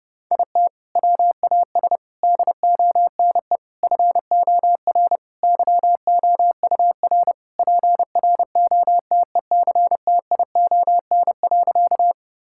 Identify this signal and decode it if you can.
Morse code, heard as a keyed tone, about 30 wpm, 700 Hz. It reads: IT WAS DONE FOR YOUR PROTECTION.
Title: Sentences from Top 1600 Words 30wpm